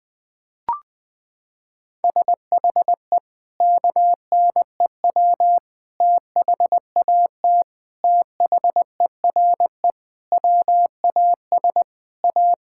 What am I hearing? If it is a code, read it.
Morse code, heard as a keyed tone, about 21 wpm, 700 Hz. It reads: SHE KNEW THAT THERE WAS A
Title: Sentences with 4-Word Phrases - 20wpm